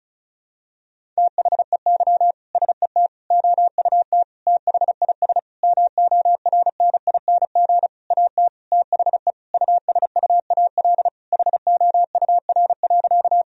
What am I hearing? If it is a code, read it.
Morse code, heard as a keyed tone, about 36 wpm, 700 Hz. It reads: THEY SET OUT THIS MORNING AT THE USUAL HOUR.
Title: Sentences from Top 1000 Words - Review Part 2 - 35wpm